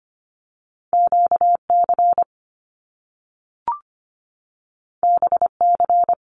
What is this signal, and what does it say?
Morse code, heard as a keyed tone, about 25 wpm, 700 Hz. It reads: QC BC
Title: Canadian Provinces and Territories 25wpm